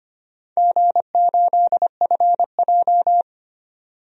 Morse code, heard as a keyed tone, about 25 wpm, 700 Hz. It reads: G8FJ
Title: DXCC-100 Call Signs - 4 Characters - 25wpm